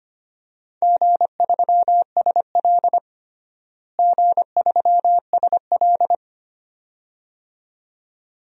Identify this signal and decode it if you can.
Morse code, heard as a keyed tone, about 25 wpm, 700 Hz. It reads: G3SL G3SL
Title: DXCC-100 Call Signs - 4 Characters - Repeated 2x - 25wpm